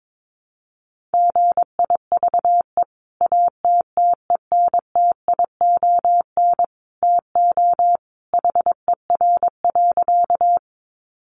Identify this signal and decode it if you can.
Morse code, heard as a keyed tone, about 22 wpm, 700 Hz. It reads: GIVE ATTENTION TO HER.
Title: Sentences from Top 1000 Words 22wpm